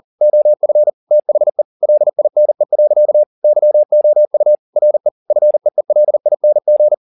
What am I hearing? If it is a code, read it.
Morse code, heard as a keyed tone, about 40 wpm, 600 Hz. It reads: OF THE LINE. YOU RE FEELING